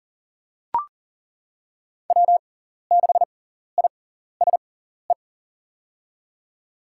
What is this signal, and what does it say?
Morse code, heard as a keyed tone, about 40 wpm, spaced as for 14 wpm, 700 Hz. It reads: W6ISE
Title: USA Call Signs - 6x Character Spacing - 40wpm